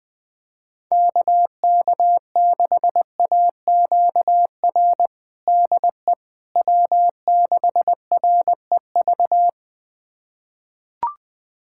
Morse code, heard as a keyed tone, about 20 wpm, 700 Hz. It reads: KK6AQR DE W6REV
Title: USA Callsign DE Callsign 20wpm